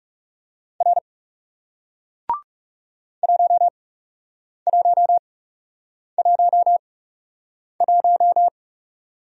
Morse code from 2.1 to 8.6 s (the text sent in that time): 1111